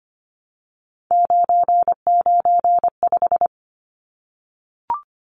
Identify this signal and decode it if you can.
Morse code, heard as a keyed tone, about 25 wpm, 700 Hz. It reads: 995
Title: Numbers - 3-Digit 25wpm